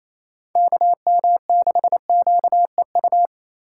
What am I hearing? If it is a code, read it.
Morse code, heard as a keyed tone, about 28 wpm, 690 Hz. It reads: KM6QEU
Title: USA Call Signs - 2x Character Spacing - 28wpm